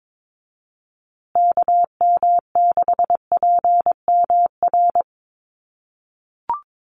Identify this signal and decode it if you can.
Morse code, heard as a keyed tone, about 22 wpm, 700 Hz. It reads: KM6PMR
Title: USA Call Signs Repeated 2x 22wpm